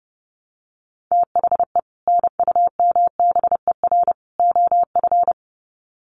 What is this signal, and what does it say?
Morse code, heard as a keyed tone, about 30 wpm, 700 Hz. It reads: THE NUMBER OF
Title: Sets of 3 Words 30wpm